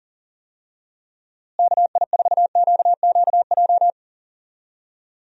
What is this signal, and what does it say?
Morse code, heard as a keyed tone, about 40 wpm, 700 Hz. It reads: KI4QQJ
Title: USA Call Signs - 6 Characters - Repeated 3x - 40wpm